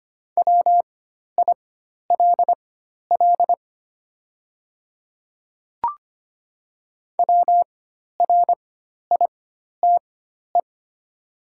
Morse code, heard as a keyed tone, about 25 wpm, spaced as for 12 wpm, 700 Hz. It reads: WILL WRITE
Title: 4x Character Spacing - Top 100 Words - 25wpm